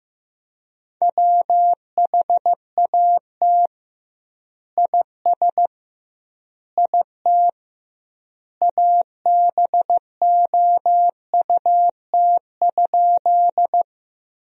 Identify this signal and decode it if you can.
Morse code, heard as a keyed tone, about 15 wpm, 700 Hz. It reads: WHAT IS IT ABOUT?
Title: 2x Word-Spacing - Sentences from Top 100 Words - 15wpm